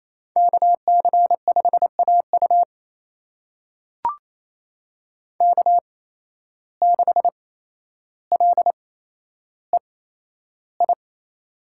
Morse code, heard as a keyed tone, about 28 wpm, spaced as for 12 wpm, 700 Hz. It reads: KC5AU K6LEI
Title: USA Call Signs - 5 Characters - 8x Character Spacing - 28wpm